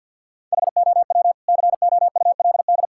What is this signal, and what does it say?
Morse code, heard as a keyed tone, about 52 wpm, 700 Hz. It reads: HOW COULD
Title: Sentences from Top 1000 Words - Review Part 6 - 50wpm